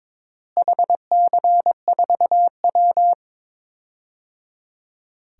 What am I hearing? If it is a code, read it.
Morse code, heard as a keyed tone, about 22 wpm, 700 Hz. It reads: HC4W
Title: DXCC-100 Call Signs - 4 Characters - Repeated 3x - 22wpm